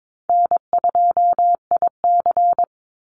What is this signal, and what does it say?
Morse code, heard as a keyed tone, about 22 wpm, 700 Hz. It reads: N2IC
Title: Easy Call Signs Repeated 3x - 22wpm